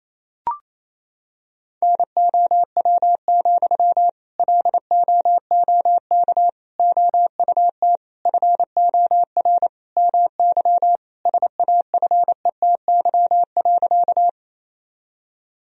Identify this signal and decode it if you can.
Morse code, heard as a keyed tone, about 28 wpm, 700 Hz. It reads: NOW, LOOK OUT FOR MY SAFETY.
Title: Sentences from Top 1000 Words - Review Part 4 - 28wpm